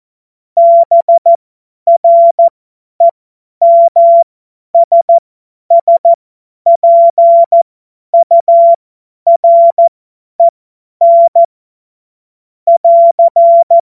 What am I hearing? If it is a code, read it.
Morse code, heard as a keyed tone, about 14 wpm, spaced as for 10 wpm, 680 Hz. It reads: BREMSSPUREN <AR>
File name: Morsekurs / Witze / Witz 16. August 2022 50 F70